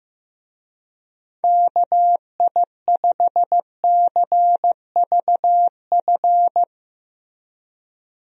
Morse code, heard as a keyed tone, about 15 wpm, 700 Hz. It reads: KI5CVF
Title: USA Call Signs - 6 Characters - Repeated 2x - 15wpm